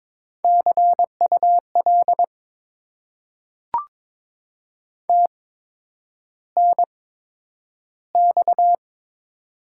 Morse code, heard as a keyed tone, about 22 wpm, spaced as for 9 wpm, 700 Hz. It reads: CUL TNX
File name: Top 90 QSO Elements - 8x Character Spacing - 22wpm